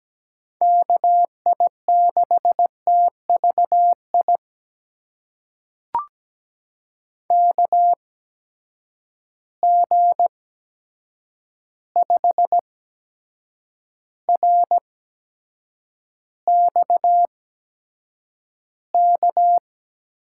Morse code, heard as a keyed tone, about 17 wpm, spaced as for 7 wpm, 700 Hz. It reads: KI6TVI KG5RXK
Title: USA Call Signs - 6 Characters - 8x Character Spacing - 17wpm